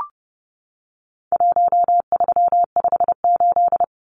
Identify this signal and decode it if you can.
Morse code, heard as a keyed tone, about 30 wpm, 700 Hz. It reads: 1358
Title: Numbers - Contest Style - 4-Digit 30wpm